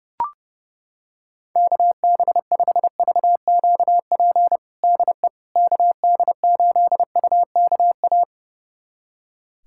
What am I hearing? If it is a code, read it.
Morse code, heard as a keyed tone, about 30 wpm, 700 Hz. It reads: KB5VQP DE KD8UKA
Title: USA Callsign DE Callsign - 6 Characters - 30wpm